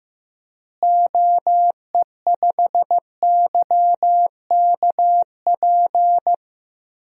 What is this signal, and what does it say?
Morse code, heard as a keyed tone, about 15 wpm, 700 Hz. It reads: OE5YKP
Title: DXCC-100 Call Signs - 6 Characters - Repeated 2x - 15wpm